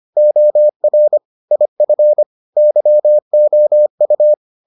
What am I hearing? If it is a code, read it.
Morse code, heard as a keyed tone, about 25 wpm, 600 Hz. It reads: OR IF YOU